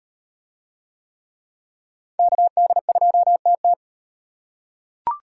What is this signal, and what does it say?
Morse code, heard as a keyed tone, about 36 wpm, 700 Hz. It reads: KD2TT